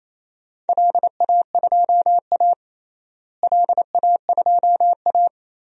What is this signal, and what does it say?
Morse code, heard as a keyed tone, about 28 wpm, 700 Hz. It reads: LA2A LA2A
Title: DXCC-100 Call Signs - 4 Characters - Repeated 2x - 28wpm